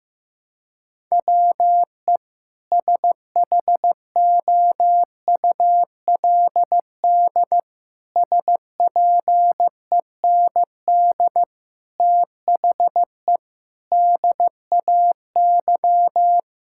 Morse code, heard as a keyed tone, about 15 wpm, 700 Hz. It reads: WE SHOULD SPEND THE DAY
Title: Sentences from Top 1000 Words - Review Part 5 - 15wpm